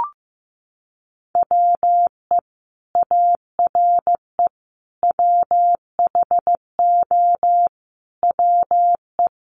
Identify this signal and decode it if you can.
Morse code, heard as a keyed tone, about 15 wpm, 700 Hz. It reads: WE ARE WHO WE
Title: Sentences from Top 200 Words 15wpm